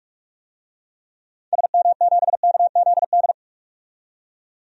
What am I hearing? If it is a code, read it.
Morse code, heard as a keyed tone, about 46 wpm, 700 Hz. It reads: SM7KZD
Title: DXCC-100 Call Signs - 6 Characters - 45wpm